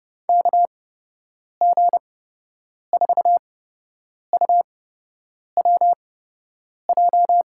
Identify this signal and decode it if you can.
Morse code, heard as a keyed tone, about 30 wpm, spaced as for 8 wpm, 700 Hz. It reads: KG4UWJ
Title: USA Call Signs - 6 Characters - 8x Character Spacing - 30wpm